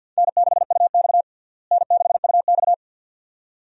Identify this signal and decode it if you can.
Morse code, heard as a keyed tone, about 52 wpm, 700 Hz. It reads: N6UX N6UX
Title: USA Call Signs - 4 Characters - Repeated 2x - 50wpm